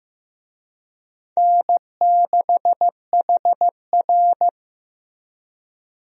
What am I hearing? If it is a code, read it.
Morse code, heard as a keyed tone, about 15 wpm, 700 Hz. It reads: N6HR
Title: USA Call Signs - 4 Characters - Repeated 2x - 15wpm